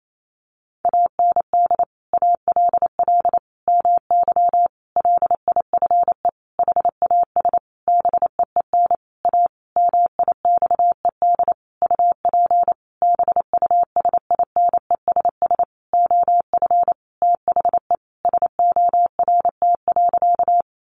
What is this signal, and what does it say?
Morse code, heard as a keyed tone, about 28 wpm, 700 Hz. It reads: AND ALL MY LIFE HAS BEEN A MIXED UP BUSINESS OF THE SORT.
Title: Sentences from Top 1500 Words 28wpm